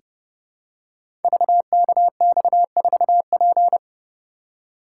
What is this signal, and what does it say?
Morse code, heard as a keyed tone, about 28 wpm, 700 Hz. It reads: VKX4P
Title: Code Groups - 5 Letter:Numbers - 30wpm